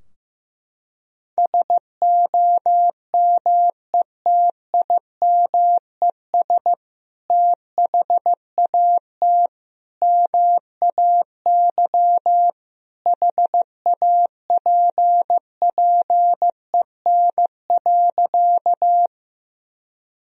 Morse code, heard as a keyed tone, about 15 wpm, 700 Hz. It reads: SOMETIMES THAT MAY HAPPEN.
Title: Sentences from Top 1000 Words - Review Part 3 - 15wpm